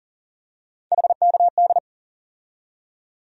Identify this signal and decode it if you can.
Morse code, heard as a keyed tone, about 40 wpm, 700 Hz. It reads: HKD
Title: USA Call Sign Postfixes - 3 Letters 40wpm